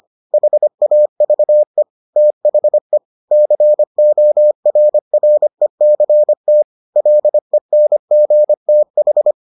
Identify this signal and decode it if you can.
Morse code, heard as a keyed tone, about 25 wpm, 600 Hz. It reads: HAVE THE CORRECT LENGTH